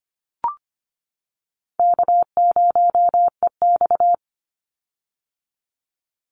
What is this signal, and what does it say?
Morse code, heard as a keyed tone, about 25 wpm, 700 Hz. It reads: K0EX
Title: Call Signs - 25wpm